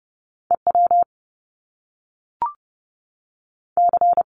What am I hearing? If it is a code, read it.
Morse code, heard as a keyed tone, about 30 wpm, 700 Hz. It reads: E W C
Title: Top Two Letter Combinations in Words 30wpm